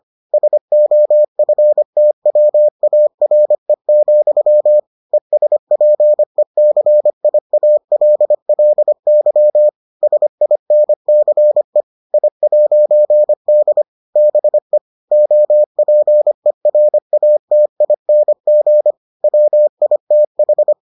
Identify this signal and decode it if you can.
Morse code, heard as a keyed tone, about 25 wpm, 600 Hz. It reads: SOFTWARE, ESPECIALLY SINCE I'D BE OPERATING WITH